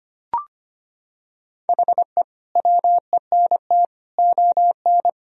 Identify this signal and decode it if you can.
Morse code, heard as a keyed tone, about 25 wpm, 700 Hz. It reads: HE WENT ON
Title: Sentences from Top 1000 Words - Review Part 3 - 25wpm